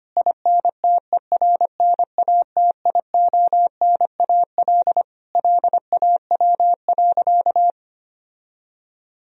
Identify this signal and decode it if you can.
Morse code, heard as a keyed tone, about 25 wpm, 700 Hz. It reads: INTERNATIONAL LAW.
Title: Sentences from Top 1000 Words - Review Part 4 - 25wpm